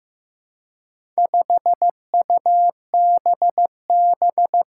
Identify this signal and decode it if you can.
Morse code, heard as a keyed tone, about 16 wpm, 700 Hz. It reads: 5UBB
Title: Four Challenging Letter-Number Combinations 15wpm